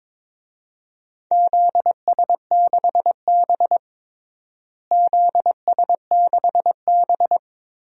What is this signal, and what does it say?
Morse code, heard as a keyed tone, about 22 wpm, 700 Hz. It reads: ZS6B ZS6B
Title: DXCC-100 Call Signs - 4 Characters - Repeated 3x - 22wpm